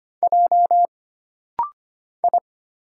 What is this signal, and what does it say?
Morse code, heard as a keyed tone, about 25 wpm, 700 Hz. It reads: J I
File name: Single Character (Letter:Number:Punctuation:Prosign) - Rapid-Fire - 25wpm